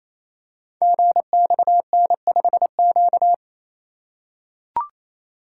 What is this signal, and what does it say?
Morse code, heard as a keyed tone, about 29 wpm, 700 Hz. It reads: GXN5Q
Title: Code Groups - 5 Letter:Numbers - 28wpm